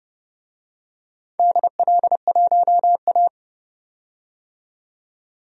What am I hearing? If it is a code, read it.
Morse code, heard as a keyed tone, about 32 wpm, 700 Hz. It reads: DL1A
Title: DXCC-100 Call Signs - 4 Characters - 30wpm